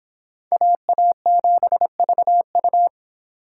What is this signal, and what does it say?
Morse code, heard as a keyed tone, about 24 wpm, 700 Hz. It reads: AA7VU